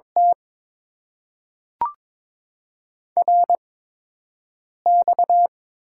Morse code, heard as a keyed tone, about 22 wpm, spaced as for 6 wpm, 700 Hz. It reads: T RX
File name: Top 90 QSO Elements - 8x Character Spacing - 22wpm